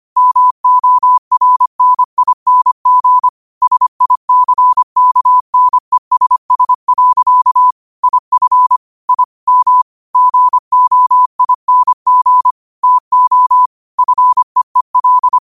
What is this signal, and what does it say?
Morse code, heard as a keyed tone, about 25 wpm, 1000 Hz. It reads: MORNING SICKNESS. IF I M GOING TO FEEL
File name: qotdfiles / 25wpm